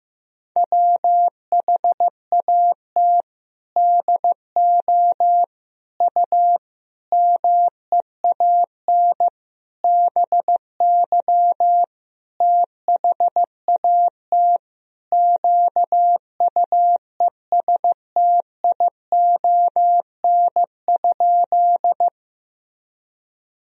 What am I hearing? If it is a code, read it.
Morse code, heard as a keyed tone, about 15 wpm, 700 Hz. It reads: WHAT DO U MEAN BY THAT QUESTION?
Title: Sentences from Top 400 Words and Abbreviations 15wpm